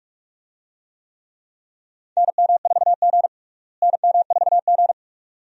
Morse code, heard as a keyed tone, about 45 wpm, 700 Hz. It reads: NM4G NM4G
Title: USA Call Signs - 4 Characters - Repeated 2x - 45wpm